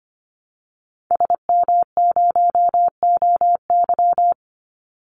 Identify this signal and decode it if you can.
Morse code, heard as a keyed tone, about 25 wpm, 700 Hz. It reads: SM0OY
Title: Call Signs Repeated 3x - 25wpm